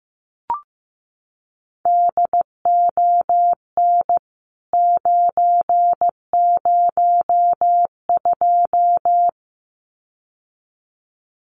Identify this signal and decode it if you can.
Morse code, heard as a keyed tone, about 15 wpm, 700 Hz. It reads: DON 902
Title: CWT Exchange 15wpm